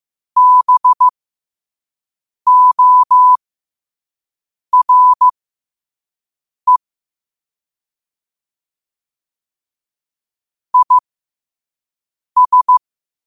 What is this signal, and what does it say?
Morse code, heard as a keyed tone, about 15 wpm, spaced as for 5 wpm, 1000 Hz. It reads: BORE IS
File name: qotdfiles / 5wpm